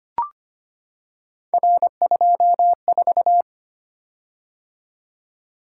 Morse code, heard as a keyed tone, about 25 wpm, 700 Hz. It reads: R24
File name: Three Letter-Number Combinations - 25wpm